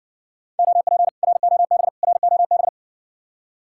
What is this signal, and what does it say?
Morse code, heard as a keyed tone, about 57 wpm, 700 Hz. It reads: OP ROB ROB